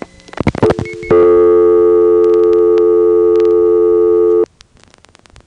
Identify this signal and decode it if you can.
Dial tone (North America), 350 + 440 Hz.